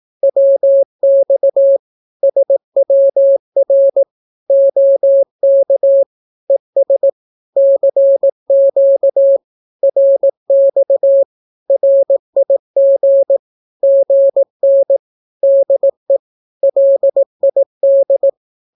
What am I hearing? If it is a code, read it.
Morse code, heard as a keyed tone, about 18 wpm, 550 Hz. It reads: WX SWR OK ES CQ RX RIG GN DE LID
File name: naukapodstaw / skroty / skroty